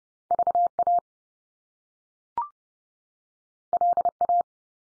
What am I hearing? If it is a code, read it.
Morse code, heard as a keyed tone, about 30 wpm, 700 Hz. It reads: VA LA
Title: US States 30WPM